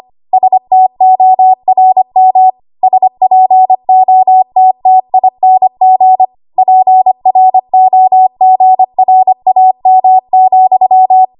Morse code, heard as a keyed tone, about 25 wpm, 750 Hz. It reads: STORM SPOTTING PROGRAM,
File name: Morse / 25WPM